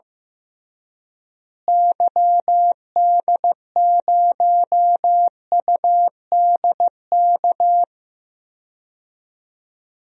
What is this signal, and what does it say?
Morse code, heard as a keyed tone, about 15 wpm, 700 Hz. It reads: YD0UDK
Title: DXCC-100 Call Signs - 6 Characters - Repeated 3x - 15wpm